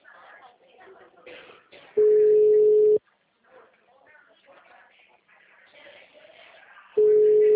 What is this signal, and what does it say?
Ringback tone (Europe, 425 Hz), with speech.